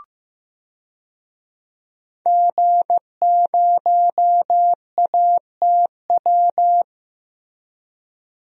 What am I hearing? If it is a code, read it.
Morse code, heard as a keyed tone, about 15 wpm, 700 Hz. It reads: G0ATW
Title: DXCC-100 Call Signs - 5 Characters - Repeated 2x - 15wpm